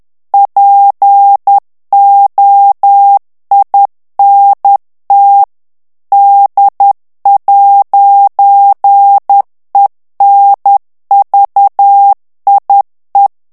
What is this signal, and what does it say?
Morse code, heard as a keyed tone, about 10 wpm, 800 Hz. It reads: POINT D'ENVIE
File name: morse mp3 / FRANCAIS TEXTE CLAIR 2 VITESSE 10 MOTS MINUTES